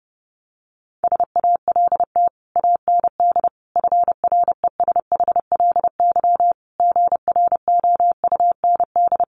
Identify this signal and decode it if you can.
Morse code, heard as a keyed tone, about 30 wpm, 700 Hz. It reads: SALT AND FRESHLY GROUND
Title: Sets of 5 Words Encore 30wpm